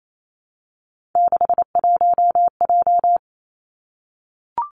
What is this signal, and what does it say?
Morse code, heard as a keyed tone, about 28 wpm, 700 Hz. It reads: 61J
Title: Three Challenging Letter-Number Combinations 28wpm